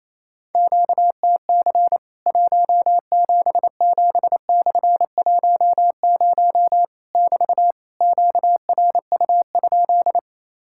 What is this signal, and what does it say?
Morse code, heard as a keyed tone, about 28 wpm, 700 Hz. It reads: QTC 177/10 <BT> QRU?
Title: Worked All Europe DX Contest - QTC Exchange - 28wpm